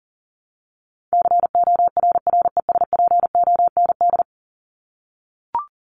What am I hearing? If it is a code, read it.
Morse code, heard as a keyed tone, about 40 wpm, 700 Hz. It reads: CORRESPOND
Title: ING Suffix - Top 100 Words with Base Words 40wpm